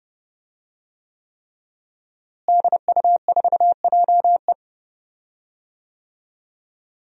Morse code, heard as a keyed tone, about 30 wpm, 700 Hz. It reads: DU4JE
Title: DXCC-100 Call Signs - 5 Characters - Repeated 3x - 30wpm